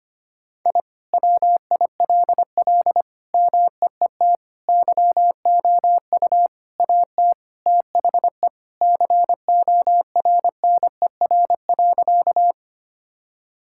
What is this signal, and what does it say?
Morse code, heard as a keyed tone, about 25 wpm, 700 Hz. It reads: I WILL MEET YOU AT THE CORNER.
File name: practice / 25wpm / All-Age Appropriate Jokes - 25wpm